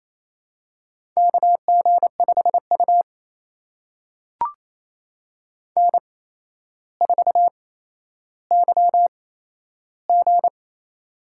Morse code, heard as a keyed tone, about 28 wpm, spaced as for 12 wpm, 700 Hz. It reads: KG5U N4YG
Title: USA Call Signs - 4 Characters - 8x Character Spacing - 28wpm